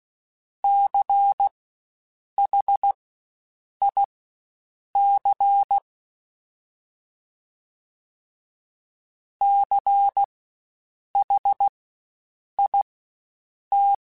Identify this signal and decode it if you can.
Morse code, heard as a keyed tone, about 16 wpm, spaced as for 7 wpm, 800 Hz. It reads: CHIC CHIT
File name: Spaced Alphas / Spaced Alphas mp3s / Cs sp @15